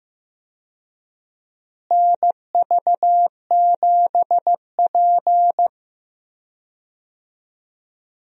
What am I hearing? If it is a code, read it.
Morse code, heard as a keyed tone, about 15 wpm, 700 Hz. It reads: NV7P
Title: USA Call Signs - 4 Characters - Repeated 3x - 15wpm